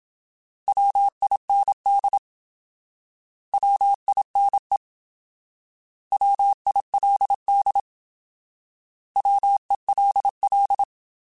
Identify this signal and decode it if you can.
Morse code, heard as a keyed tone, about 26 wpm, 800 Hz. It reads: WIND WINE WILD WELL
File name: Alphabetical / Alphabetical mp3s / Ws @25